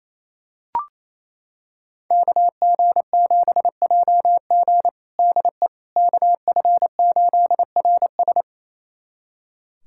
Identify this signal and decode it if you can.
Morse code, heard as a keyed tone, about 28 wpm, 700 Hz. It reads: KG7JG DE KF8RS
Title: USA Callsign DE Callsign - 5 Characters - 28wpm